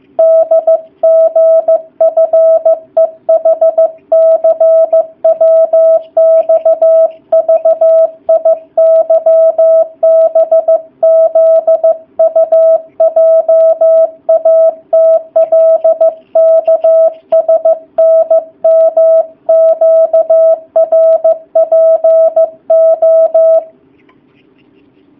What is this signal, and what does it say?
Morse code, heard as a keyed tone, about 15 wpm, 650 Hz. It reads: DGFEHCWXVIYBZUJATLKSNMQRPO
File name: jh8cbh / 2019cw / sp15- w